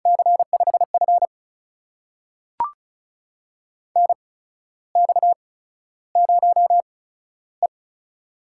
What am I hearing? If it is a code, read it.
Morse code, heard as a keyed tone, about 36 wpm, spaced as for 14 wpm, 700 Hz. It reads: C5F NX0E
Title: USA Call Signs - 4 Characters - 8x Character Spacing - 35wpm